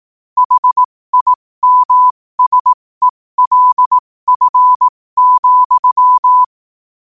Morse code, heard as a keyed tone, about 18 wpm, spaced as for 15 wpm, 1000 Hz. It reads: HIMSELF,
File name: qotdfiles / 15wpm